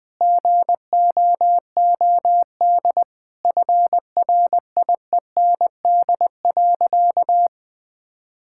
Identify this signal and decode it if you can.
Morse code, heard as a keyed tone, about 20 wpm, 700 Hz. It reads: GOOD FRIEND.
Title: Sentences from Top 1600 Words 20wpm